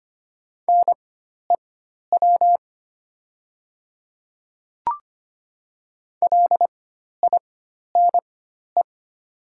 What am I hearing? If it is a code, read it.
Morse code, heard as a keyed tone, about 25 wpm, spaced as for 12 wpm, 690 Hz. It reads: NEW LINE
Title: 4x Character Spacing - Top 200 Words - 25wpm